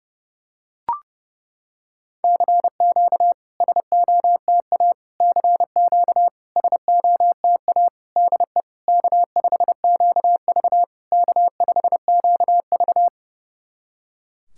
Morse code, heard as a keyed tone, about 30 wpm, 690 Hz. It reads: CQ SOTA CQ SOTA DE K5QV K5QV
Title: SOTA QSO Practice 30wpm